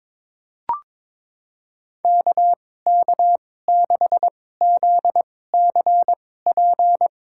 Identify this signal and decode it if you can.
Morse code, heard as a keyed tone, about 22 wpm, spaced as for 16 wpm, 700 Hz. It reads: KK6ZCP
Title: USA Call Signs - 6 Characters - 2x Character Spacing - 22wpm